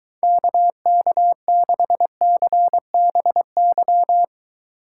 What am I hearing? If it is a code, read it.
Morse code, heard as a keyed tone, about 23 wpm, 700 Hz. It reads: KK6CBY